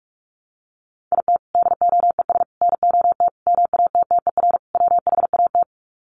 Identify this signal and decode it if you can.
Morse code, heard as a keyed tone, about 44 wpm, 700 Hz. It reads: IT DOES NOT MATTER WHAT
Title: Sets of 5 Words Encore 45wpm